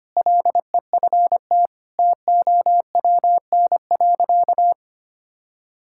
Morse code, heard as a keyed tone, about 25 wpm, 700 Hz. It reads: LEFT TOWN.
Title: Sentences from Top 1000 Words - Review Part 6 - 25wpm